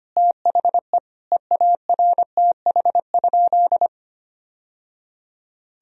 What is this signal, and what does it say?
Morse code, heard as a keyed tone, about 25 wpm, 700 Hz. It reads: THE EARTH?
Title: Sentences from Top 1000 Words - Review Part 2 - 25wpm